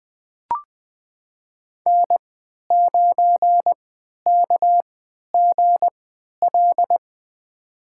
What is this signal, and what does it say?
Morse code, heard as a keyed tone, about 20 wpm, spaced as for 11 wpm, 700 Hz. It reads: N9KGL
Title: USA Call Signs - 5 Characters - 3x Character Spacing - 20wpm